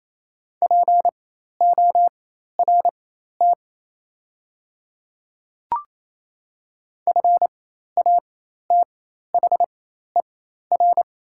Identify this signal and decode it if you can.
Morse code, heard as a keyed tone, about 28 wpm, spaced as for 13 wpm, 690 Hz. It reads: PORT FATHER
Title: 4x Character Spacing - Top 200 Words - 28wpm